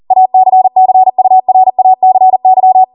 Morse code, heard as a keyed tone, about 40 wpm, 750 Hz. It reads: ACCURACY